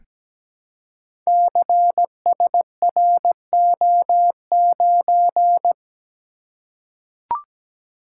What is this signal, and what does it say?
Morse code, heard as a keyed tone, about 17 wpm, 700 Hz. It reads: CSRO9
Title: Code Groups - 5 Letter:Numbers - 17wpm